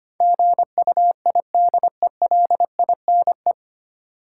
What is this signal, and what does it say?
Morse code, heard as a keyed tone, about 25 wpm, 700 Hz. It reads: GUIDELINE
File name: Nine Letter Words - 25wpm